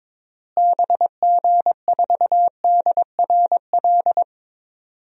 Morse code, heard as a keyed tone, about 22 wpm, 700 Hz. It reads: BG4DRL
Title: DXCC-100 Call Signs - 6 Characters - Repeated 3x - 22wpm